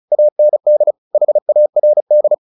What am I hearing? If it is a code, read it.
Morse code, heard as a keyed tone, about 34 wpm, 600 Hz. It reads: AND HARD
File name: the old man and the sea 2 35wpm